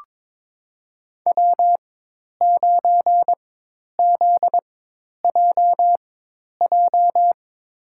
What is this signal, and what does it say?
Morse code, heard as a keyed tone, about 22 wpm, spaced as for 10 wpm, 700 Hz. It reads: W9ZJJ
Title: USA Call Signs - 5 Characters - 4x Character Spacing - 22wpm